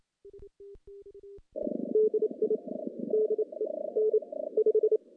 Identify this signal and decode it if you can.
Morse code, heard as a keyed tone, about 27 wpm, 430 Hz. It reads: DI DE N 5
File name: contests / CW-SS-k5tr- -1400